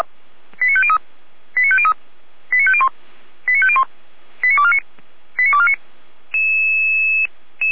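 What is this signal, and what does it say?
Five-tone selective call (ZVEI), heard as quick runs of five tones, tones of about 70 ms. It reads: 89503 89503 89501 89501 89249 89249
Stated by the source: ZVEI five-tone selective call received by SDR.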